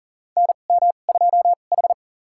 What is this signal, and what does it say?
Morse code, heard as a keyed tone, about 40 wpm, spaced as for 29 wpm, 700 Hz. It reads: NM2H
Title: USA Call Signs - 4 Characters - 2x Character Spacing - 40wpm